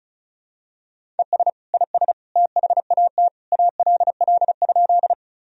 Morse code, heard as a keyed tone, about 34 wpm, 700 Hz. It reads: ES IS THAT ALL?
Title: Sentences from Top 100 Words and Abbreviations 35wpm